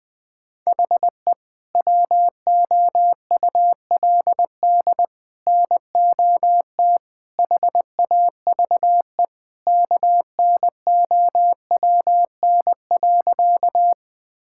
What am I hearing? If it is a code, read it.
Morse code, heard as a keyed tone, about 21 wpm, 700 Hz. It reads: HE WOULD NOT HAVE KNOWN.
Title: Sentences with 3-Word Phrases - Spoken Beginning - 20wpm